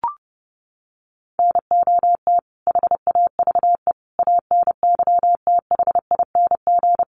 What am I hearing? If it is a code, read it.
Morse code, heard as a keyed tone, about 28 wpm, 700 Hz. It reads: NOT HAVE ANYTHING
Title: Sets of 5 Words Encore 30wpm